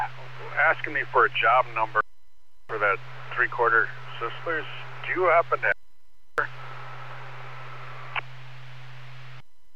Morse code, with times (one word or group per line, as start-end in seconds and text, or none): none